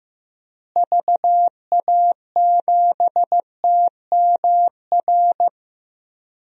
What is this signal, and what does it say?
Morse code, heard as a keyed tone, about 15 wpm, 700 Hz. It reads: VA7TMR
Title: DXCC-100 Call Signs - 6 Characters - Repeated 3x - 15wpm